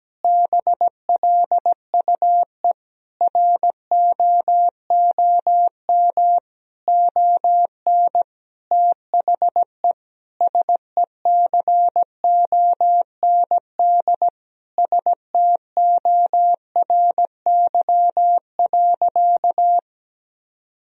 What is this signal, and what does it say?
Morse code, heard as a keyed tone, about 17 wpm, 700 Hz. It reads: BLUE ROOM ON THE SECOND STORY.
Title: Sentences from Top 1000 Words - Review Part 3 - 17wpm